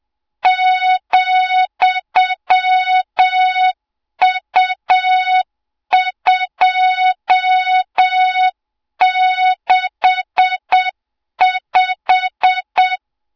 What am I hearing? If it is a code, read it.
Morse code, heard as a keyed tone, about 8 wpm, 3700 Hz, with noise. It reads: ,U265
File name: Morze-HA2MN / hang / 20 GYAKORLAT